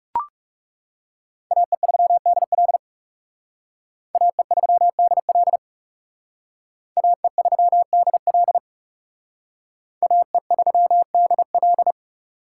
Morse code, heard as a keyed tone, about 36 wpm, 700 Hz. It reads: AE3DL AE3DL AE3DL AE3DL
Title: Speed Racing - USA 2x2 Call Signs 30wpm